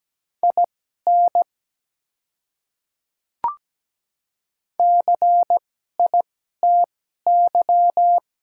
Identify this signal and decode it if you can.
Morse code, heard as a keyed tone, about 17 wpm, spaced as for 12 wpm, 700 Hz. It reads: IN CITY
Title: 2x Character Spacing - Top 300 Words - 17wpm